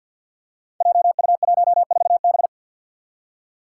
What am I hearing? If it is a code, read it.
Morse code, heard as a keyed tone, about 48 wpm, 700 Hz. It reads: JU14B
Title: Five Challenging Letter-Number Combinations 50wpm